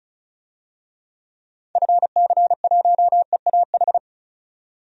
Morse code, heard as a keyed tone, about 36 wpm, 700 Hz. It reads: FC1EAH
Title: DXCC-100 Call Signs - 6 Characters - 35wpm